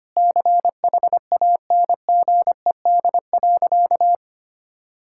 Morse code, heard as a keyed tone, about 25 wpm, 700 Hz. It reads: CHANGED.
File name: practice / 25wpm / Sentences with 3-Word Phrases - 25wpm